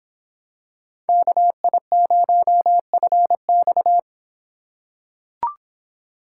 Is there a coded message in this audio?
Yes: Morse code from 1.1 to 4.0 s, heard as a keyed tone, about 24 wpm, 700 Hz. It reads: KI0FX